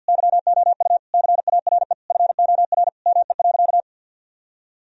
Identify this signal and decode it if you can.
Morse code, heard as a keyed tone, about 53 wpm, 700 Hz. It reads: YOU CARE FOR ME.
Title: Sentences from Top 1000 Words - Review Part 1 - 50wpm